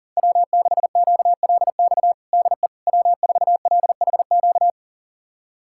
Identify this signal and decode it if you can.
Morse code, heard as a keyed tone, about 37 wpm, 700 Hz. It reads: W6QLX DE W4LHQ
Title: USA Callsign DE Callsign - 5 Characters - 40wpm